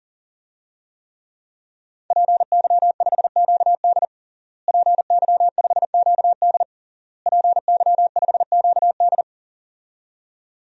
Morse code, heard as a keyed tone, about 37 wpm, 700 Hz. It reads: PY5QD PY5QD PY5QD
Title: DXCC-100 Call Signs - 5 Characters - Repeated 3x - 40wpm